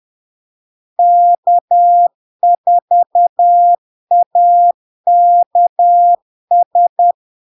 Morse code, heard as a keyed tone, about 10 wpm, 700 Hz. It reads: K4AKS
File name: LARK-FD-2022-Log-in-CW-10wpm-700-Hz